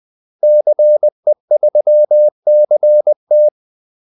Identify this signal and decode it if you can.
Morse code, heard as a keyed tone, about 20 wpm, 600 Hz. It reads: CE3CT
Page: Kaldesignaler 20 wpm